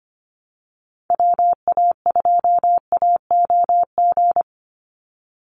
Morse code, heard as a keyed tone, about 25 wpm, 700 Hz. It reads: WA2AOG
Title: Call Signs - 25wpm